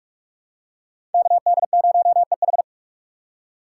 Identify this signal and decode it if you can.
Morse code, heard as a keyed tone, about 44 wpm, 700 Hz. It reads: KD0EH
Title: USA Call Signs - 4x Character Spacing - 45wpm